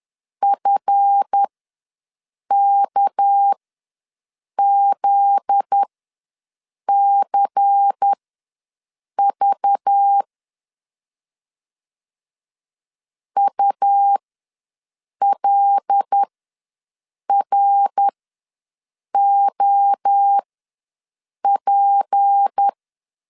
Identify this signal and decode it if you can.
Morse code, heard as a keyed tone, about 10 wpm, spaced as for 6 wpm, 800 Hz. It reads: FKZCV ULROP
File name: LEZIONI / 8 YJ45 / 8 Y ws 10wpm 3952